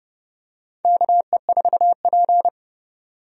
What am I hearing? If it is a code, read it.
Morse code, heard as a keyed tone, about 30 wpm, 700 Hz. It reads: KE4P
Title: USA Call Signs - 4 Characters - 30wpm